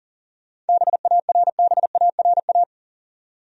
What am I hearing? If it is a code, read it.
Morse code, heard as a keyed tone, about 40 wpm, 700 Hz. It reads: BARBARA
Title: USA Common Names - 40wpm